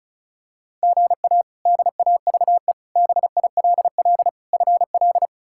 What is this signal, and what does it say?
Morse code, heard as a keyed tone, about 34 wpm, 700 Hz. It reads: GA DAVE BILL FL
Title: K1USN SST QSO Practice - 35wpm